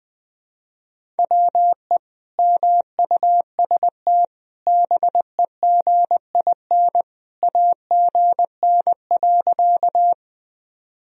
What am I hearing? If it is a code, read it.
Morse code, heard as a keyed tone, about 19 wpm, 700 Hz. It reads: WE MUST BEGIN AGN.
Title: Sentences from Top 300 Words and Abbreviations 20wpm